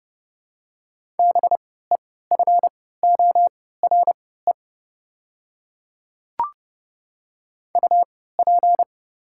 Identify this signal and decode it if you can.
Morse code, heard as a keyed tone, about 30 wpm, spaced as for 17 wpm, 700 Hz. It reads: BEFORE UP
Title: 3x Character Spacing - Top 500 Words - 30wpm